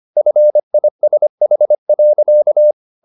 Morse code, heard as a keyed tone, about 25 wpm, 600 Hz. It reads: FISH.